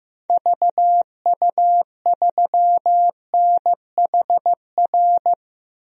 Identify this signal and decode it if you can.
Morse code, heard as a keyed tone, about 15 wpm, 700 Hz. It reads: VU3NHR
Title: DXCC-100 Call Signs - 6 Characters - 15wpm